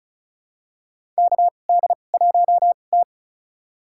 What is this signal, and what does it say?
Morse code, heard as a keyed tone, about 34 wpm, spaced as for 25 wpm, 700 Hz. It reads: KD1T
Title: USA Call Signs - 4 Characters - 2x Character Spacing - 35wpm